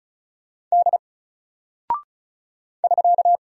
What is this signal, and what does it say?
Morse code, heard as a keyed tone, about 34 wpm, 700 Hz. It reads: D <SK>
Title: Single Character (Letter:Number:Punctuation:Prosign) 35wpm